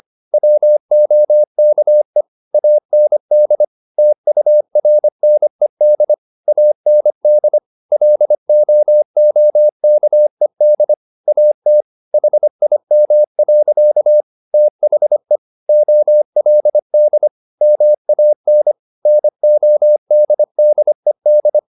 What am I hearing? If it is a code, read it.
Morse code, heard as a keyed tone, about 25 wpm, 600 Hz. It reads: WOKE AND TURNED AND LOOKED AT HIM. THE OLD MAN NODDED
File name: the old man and the sea part 1 25wpm